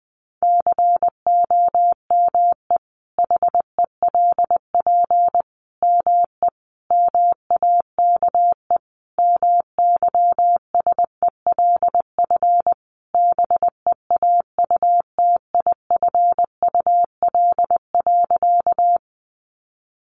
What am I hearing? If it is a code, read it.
Morse code, heard as a keyed tone, about 20 wpm, 700 Hz. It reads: COME HELP ME MAKE MYSELF BEAUTIFUL.
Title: Sentences from Top 1300 Words 20wpm